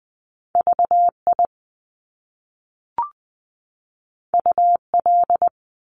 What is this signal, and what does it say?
Morse code, heard as a keyed tone, about 20 wpm, 700 Hz. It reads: VI UL
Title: Top Two Letter Combinations in Words 20wpm